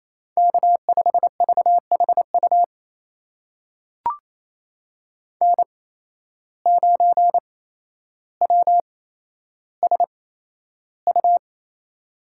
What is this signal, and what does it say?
Morse code, heard as a keyed tone, about 28 wpm, spaced as for 12 wpm, 700 Hz. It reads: K5VHU N9WSU
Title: USA Call Signs - 8x Character Spacing - 28wpm